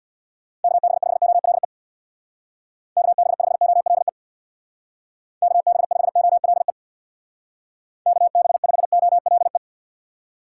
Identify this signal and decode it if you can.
Morse code, heard as a keyed tone, about 60 wpm, 700 Hz. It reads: KB5OLE KB5OLE KB5OLE KB5OLE
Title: Speed Racing - USA 2x3 Call Signs 50wpm